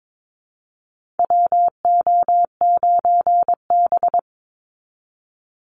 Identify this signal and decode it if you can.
Morse code, heard as a keyed tone, about 22 wpm, 700 Hz. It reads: WO9B
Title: Call Signs Repeated 3x - 22wpm